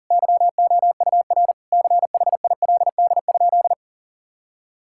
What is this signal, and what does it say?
Morse code, heard as a keyed tone, about 37 wpm, 700 Hz. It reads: YOUR CHILD?
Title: Sentences from Top 1000 Words - Review Part 6 - 40wpm